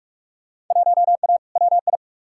Morse code, heard as a keyed tone, about 47 wpm, 700 Hz. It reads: 1A WI
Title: Field Day Exchanges - 45wpm